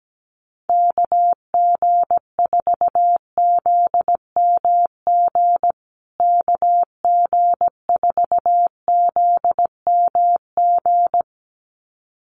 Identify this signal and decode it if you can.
Morse code, heard as a keyed tone, about 17 wpm, 700 Hz. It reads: KG4ZMG KG4ZMG
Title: USA Call Signs Repeated 2x 17wpm